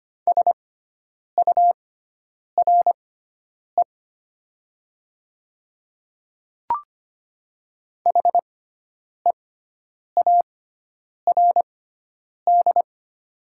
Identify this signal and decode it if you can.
Morse code, heard as a keyed tone, about 25 wpm, spaced as for 9 wpm, 700 Hz. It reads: SURE HEARD
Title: 6x Character Spacing - Top 500 Words - 25wpm